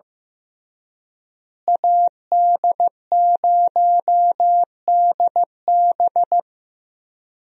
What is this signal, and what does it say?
Morse code, heard as a keyed tone, about 15 wpm, 700 Hz. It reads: AD0DB
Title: USA Call Signs - 5 Characters - Repeated 2x - 15wpm